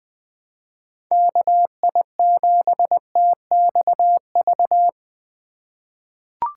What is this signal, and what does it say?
Morse code, heard as a keyed tone, about 20 wpm, 700 Hz. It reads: KI7TXV
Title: USA Call Signs - 4x Character Spacing - 20wpm